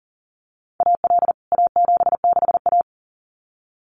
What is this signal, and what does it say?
Morse code, heard as a keyed tone, about 40 wpm, 700 Hz. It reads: AL A76A
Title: CWT Exchange 40wpm